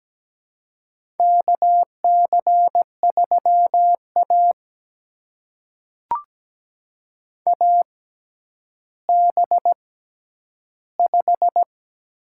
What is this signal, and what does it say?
Morse code, heard as a keyed tone, about 17 wpm, spaced as for 10 wpm, 700 Hz. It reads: KC3A AB5
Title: USA Call Signs - 4 Characters - 6x Character Spacing - 17wpm